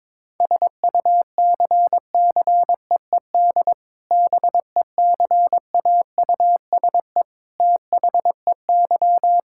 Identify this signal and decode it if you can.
Morse code, heard as a keyed tone, about 22 wpm, 700 Hz. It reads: SUCCEED BECAUSE THEY
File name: practice / 22wpm / Sentences with 2-Word Phrases - 22wpm